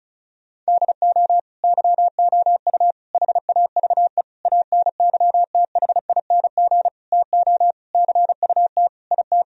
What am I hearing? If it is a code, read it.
Morse code, heard as a keyed tone, about 36 wpm, 700 Hz. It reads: DO YOU HAVE ANYTHING TO CUT IT
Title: Sentences from Top 1000 Words - Review Part 1 - 35wpm